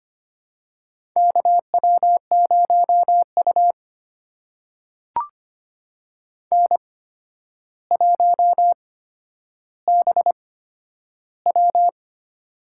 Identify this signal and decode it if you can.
Morse code, heard as a keyed tone, about 25 wpm, spaced as for 11 wpm, 700 Hz. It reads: KW0U N1BW
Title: USA Call Signs - 4 Characters - 8x Character Spacing - 25wpm